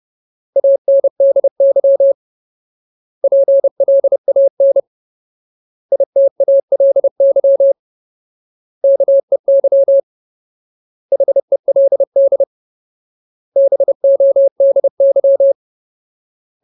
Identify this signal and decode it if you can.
Morse code, heard as a keyed tone, about 30 wpm, 550 Hz. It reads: ANDY PLAN ITALY KEY HELD BODY
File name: Ultimate Word List / 30WPM Ultimate